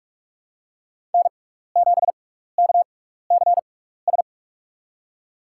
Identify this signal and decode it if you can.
Morse code, heard as a keyed tone, about 46 wpm, spaced as for 15 wpm, 700 Hz. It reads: N7KCS
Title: USA Call Signs - 5 Characters - 6x Character Spacing - 45wpm